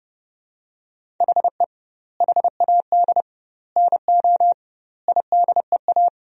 Morse code, heard as a keyed tone, about 32 wpm, 700 Hz. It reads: HE HAD NO IDEA
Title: 2x Word-Spacing - Sentences from Top 300 Words - 30wpm